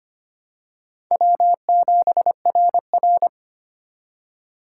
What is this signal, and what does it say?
Morse code, heard as a keyed tone, about 25 wpm, 700 Hz. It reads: W7RR
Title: USA Call Signs - 4 Characters - 4x Character Spacing - 25wpm